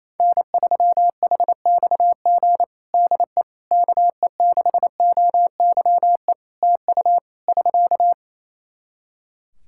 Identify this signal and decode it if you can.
Morse code, heard as a keyed tone, about 28 wpm, 700 Hz. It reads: N3HXG DE KE6OYE TU <SK>
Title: Practice QSO - Exchange 4 - Ending - 28wpm